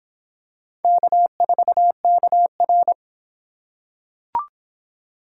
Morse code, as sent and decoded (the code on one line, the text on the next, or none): -.- ....- -.- .-.
K4KR